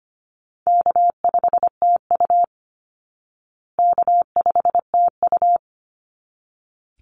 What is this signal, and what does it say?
Morse code, heard as a keyed tone, about 25 wpm, 700 Hz. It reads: K5TU K5TU
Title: Easy Call Signs Repeated 2x - 25wpm